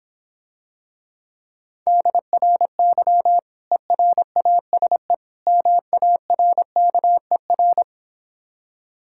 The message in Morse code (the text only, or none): DRY ERASE MARKER